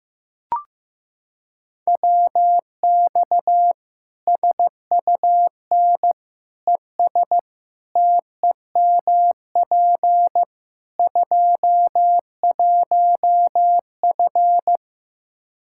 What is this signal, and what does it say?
Morse code, heard as a keyed tone, about 15 wpm, 700 Hz. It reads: WX SUN ES TEMP 21F
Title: Practice QSO - Exchange 2 - Info - 15wpm